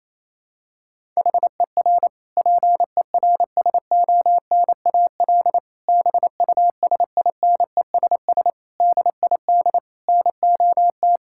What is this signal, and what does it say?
Morse code, heard as a keyed tone, about 28 wpm, 700 Hz. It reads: HER PERSONAL BUSINESS DID NOT
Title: Sentences from Top 1000 Words - Review Part 6 - 28wpm